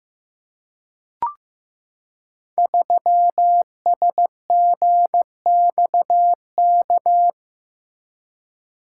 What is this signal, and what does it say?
Morse code, heard as a keyed tone, about 15 wpm, 700 Hz. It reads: 3SGXK
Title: Code Groups - 5 Letter:Numbers - 15wpm